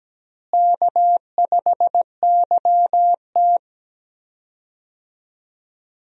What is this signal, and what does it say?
Morse code, heard as a keyed tone, about 17 wpm, 700 Hz. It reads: K5YT
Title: USA Call Signs - 4 Characters - Repeated 2x - 17wpm